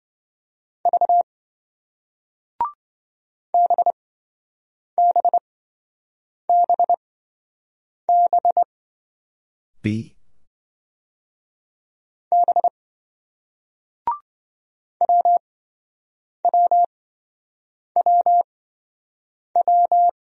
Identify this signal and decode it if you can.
Morse code, heard as a keyed tone, about 27 wpm, spaced as for 7 wpm, 700 Hz. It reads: V BBBB B WWWW